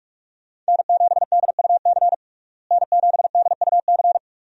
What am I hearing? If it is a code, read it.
Morse code, heard as a keyed tone, about 42 wpm, 700 Hz. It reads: N7DUC N7DUC
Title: USA Call Signs - 5 Characters - Repeated 3x - 45wpm